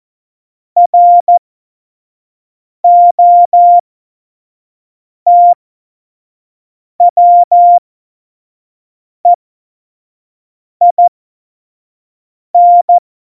Morse code, heard as a keyed tone, about 14 wpm, spaced as for 5 wpm, 700 Hz. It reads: ROTWEIN
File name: Morsekurs / Witze / Witz 03.Dezember 2019, 25 F70